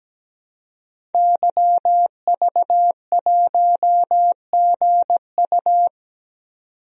Morse code, heard as a keyed tone, about 17 wpm, 700 Hz. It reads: YV1GU
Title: DXCC-100 Call Signs - 5 Characters - 17wpm